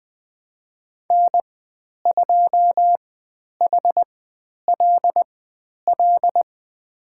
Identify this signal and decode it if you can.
Morse code, heard as a keyed tone, about 20 wpm, spaced as for 10 wpm, 700 Hz. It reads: N2HLL
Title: 109 Lesson 18g - Letter m - Call Signs 20wpm